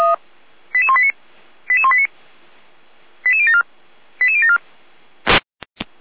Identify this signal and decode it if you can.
Five-tone selective call (ZVEI), heard as quick runs of five tones, tones of about 70 ms. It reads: 88189 88189 88074 88074